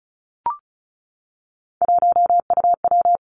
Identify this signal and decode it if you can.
Morse code, heard as a keyed tone, about 36 wpm, 700 Hz. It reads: 1UW
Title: Three Challenging Letter-Number Combinations 35wpm